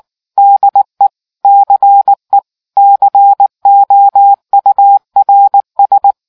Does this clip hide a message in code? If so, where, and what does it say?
Yes: Morse code from 0.4 to 6.1 s, heard as a keyed tone, about 19 wpm, 800 Hz. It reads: DE CE COURS